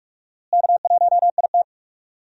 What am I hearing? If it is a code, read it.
Morse code, heard as a keyed tone, about 42 wpm, 700 Hz. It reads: K1IT
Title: USA Call Signs - 4 Characters - Repeated 2x - 45wpm